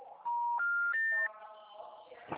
Special information tone, with noise.
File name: out- -2018- - - .17303